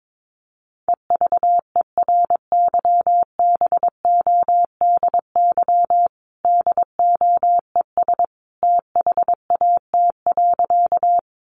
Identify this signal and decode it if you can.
Morse code, heard as a keyed tone, about 22 wpm, 700 Hz. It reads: EVERYBODY DOES THAT.
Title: Sentences from Top 1400 Words 22wpm